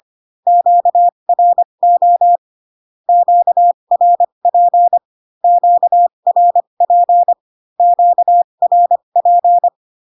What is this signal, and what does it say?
Morse code, heard as a keyed tone, about 25 wpm, spaced as for 22 wpm, 700 Hz. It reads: QRO QRP QRP QRP
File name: Q-codes most common repeated 3 times 18 25wpm